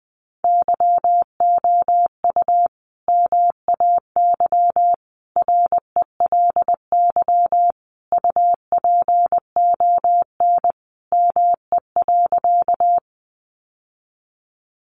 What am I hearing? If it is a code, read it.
Morse code, heard as a keyed tone, about 20 wpm, 700 Hz. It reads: YOU MAY RELY UPON ME.
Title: Sentences from Top 1200 Words 20wpm